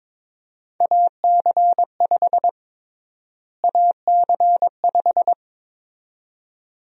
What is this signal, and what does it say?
Morse code, heard as a keyed tone, about 22 wpm, 700 Hz. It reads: AC5 AC5
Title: USA Call Sign Prefixes - Repeated 2x - 22wpm